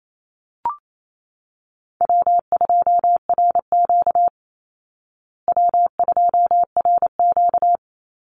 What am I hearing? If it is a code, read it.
Morse code, heard as a keyed tone, about 28 wpm, 700 Hz. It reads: W2RQ W2RQ
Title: Call Signs Repeated 2x - 28wpm